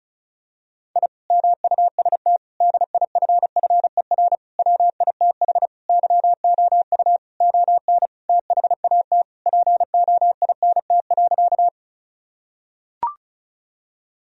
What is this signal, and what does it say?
Morse code, heard as a keyed tone, about 36 wpm, 700 Hz. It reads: I MUST DIFFER WITH YOU ON THAT POINT.
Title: Sentences from Top 1000 Words - Review Part 1 - 35wpm